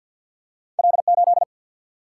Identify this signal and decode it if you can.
Morse code, heard as a keyed tone, about 47 wpm, 700 Hz. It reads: L 8
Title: Single-Letter Single-Number 50wpm